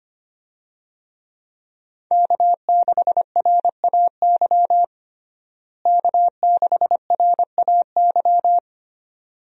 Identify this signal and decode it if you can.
Morse code, heard as a keyed tone, about 25 wpm, 700 Hz. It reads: K6RAY K6RAY
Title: USA Call Signs - 5 Characters - Repeated 2x - 25wpm